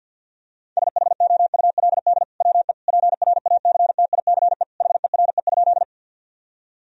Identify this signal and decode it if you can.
Morse code, heard as a keyed tone, about 50 wpm, 700 Hz. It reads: SHOULD WE PRACTICE HERE?
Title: Sentences from Top 1000 Words - Review Part 4 - 50wpm